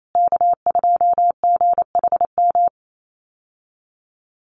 Morse code, heard as a keyed tone, about 28 wpm, 700 Hz. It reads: K2GHM
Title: USA Call Signs Repeated 3x 28wpm